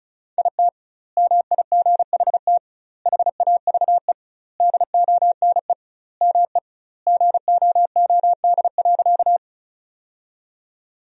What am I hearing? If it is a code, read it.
Morse code, heard as a keyed tone, about 36 wpm, 700 Hz. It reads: IT MIGHT HAVE DONE ME GOOD.
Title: 2x Word-Spacing - Sentences from Top 500 Words - 35wpm